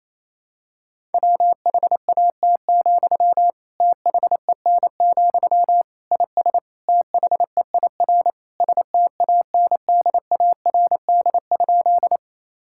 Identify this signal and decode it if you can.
Morse code, heard as a keyed tone, about 27 wpm, 700 Hz. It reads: WHAT, THEN, IS THEIR STANDARD?
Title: Sentences from Top 1000 Words - Review Part 6 - 28wpm